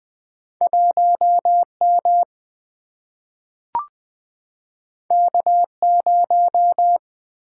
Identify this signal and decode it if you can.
Morse code, heard as a keyed tone, about 20 wpm, 700 Hz. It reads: 1M K0
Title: Single-Letter Single-Number 20wpm